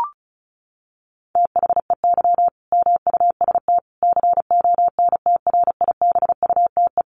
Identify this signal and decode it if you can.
Morse code, heard as a keyed tone, about 34 wpm, 700 Hz. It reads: THEY MUST CONTRIBUTE
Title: Sentences from Top 1400 Words 35wpm